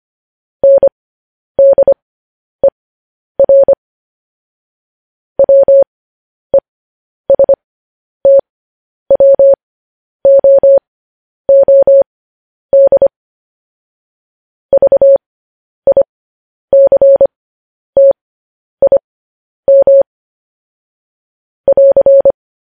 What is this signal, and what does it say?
Morse code, heard as a keyed tone, about 25 wpm, spaced as for 10 wpm, 550 Hz. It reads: NDER WESTWOOD VICTIM <AR>
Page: news headlines in morse code at 10 wpm